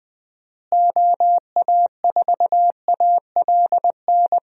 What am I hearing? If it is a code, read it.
Morse code, heard as a keyed tone, about 20 wpm, 700 Hz. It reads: OA4ALN
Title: USA Call Signs - 6 Characters - Repeated 2x - 20wpm